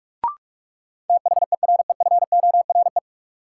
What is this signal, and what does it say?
Morse code, heard as a keyed tone, about 45 wpm, 700 Hz. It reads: THEREFORE
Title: Sentences from Top 800 Words - New Word Drill - 45wpm